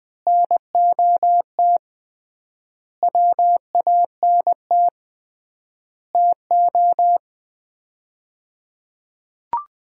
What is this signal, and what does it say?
Morse code, heard as a keyed tone, about 20 wpm, 700 Hz. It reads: NOT WANT TO
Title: 3x Word-Spacing - Sets of 5 Words - 20wpm